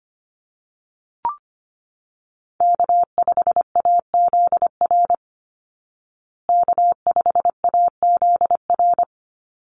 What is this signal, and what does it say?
Morse code, heard as a keyed tone, about 25 wpm, 700 Hz. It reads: K5AZR K5AZR
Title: Call Signs Repeated 2x - 25wpm